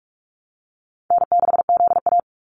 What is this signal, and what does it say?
Morse code, heard as a keyed tone, about 44 wpm, 700 Hz. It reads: N6ZA
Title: NA SOTA Activator Callsigns 45wpm